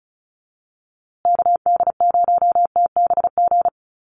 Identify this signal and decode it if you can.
Morse code, heard as a keyed tone, about 34 wpm, 700 Hz. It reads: KD0TBG
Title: USA Call Signs Repeated 3x 35wpm